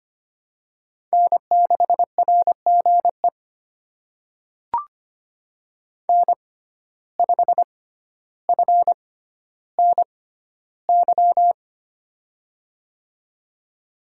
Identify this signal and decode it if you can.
Morse code, heard as a keyed tone, about 25 wpm, spaced as for 13 wpm, 700 Hz. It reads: N6RGE N5FNY
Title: USA Call Signs - 5 Characters - 6x Character Spacing - 25wpm